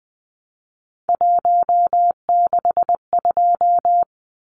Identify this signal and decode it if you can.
Morse code, heard as a keyed tone, about 20 wpm, 700 Hz. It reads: 162
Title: Numbers - 3-Digit 20wpm